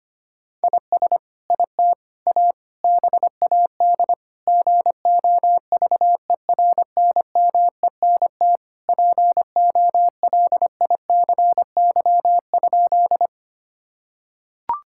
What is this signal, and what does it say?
Morse code, heard as a keyed tone, about 25 wpm, 700 Hz. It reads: IS IT A BAD GOVERNMENT POLICY?
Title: Sentences from Top 1000 Words - Review Part 5 - 25wpm